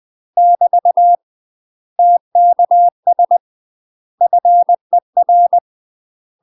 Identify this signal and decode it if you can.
Morse code, heard as a keyed tone, about 20 wpm, 700 Hz. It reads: <BT> TKS FER
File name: UK MASTER QSOs TRAINING / 20WPM / UK MASTER QSOs 0025